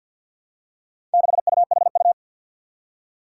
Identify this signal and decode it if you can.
Morse code, heard as a keyed tone, about 53 wpm, 700 Hz. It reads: 6UHU
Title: Four Challenging Letter-Number Combinations 50wpm